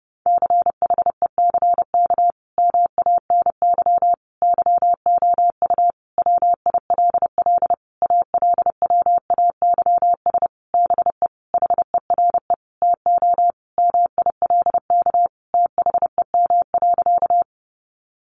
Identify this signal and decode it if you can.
Morse code, heard as a keyed tone, about 30 wpm, 700 Hz. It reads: CHECK MANY YOU WILL ALWAYS BE HERE TO MILK THEM.
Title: Sentences from Top 1500 Words 30wpm